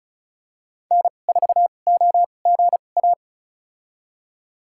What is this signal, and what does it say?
Morse code, heard as a keyed tone, about 36 wpm, spaced as for 25 wpm, 700 Hz. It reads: N4OGA
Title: USA Call Signs - 5 Characters - 2x Character Spacing - 35wpm